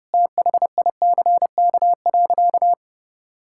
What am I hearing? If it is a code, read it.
Morse code, heard as a keyed tone, about 32 wpm, 700 Hz. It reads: THICK.
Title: Sentences from Top 1600 Words 30wpm